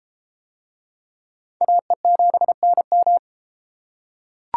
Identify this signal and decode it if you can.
Morse code, heard as a keyed tone, about 31 wpm, 700 Hz. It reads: AE7NM